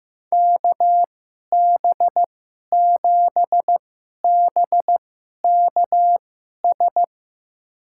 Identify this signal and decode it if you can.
Morse code, heard as a keyed tone, about 15 wpm, spaced as for 11 wpm, 700 Hz. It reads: KB7BKS
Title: USA Call Signs - 2x Character Spacing - 15wpm